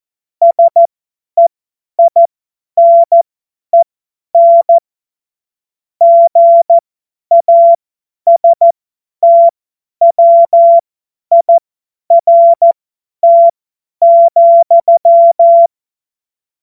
Morse code, heard as a keyed tone, about 14 wpm, spaced as for 10 wpm, 680 Hz. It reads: SEINEN GASTWIRT,